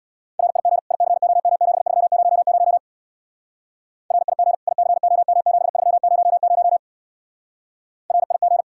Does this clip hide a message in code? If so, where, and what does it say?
Yes: Morse code from 0.4 to 8.6 s, heard as a keyed tone, about 64 wpm, 700 Hz. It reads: RIG ICOM7300 RIG ICOM7300 RIG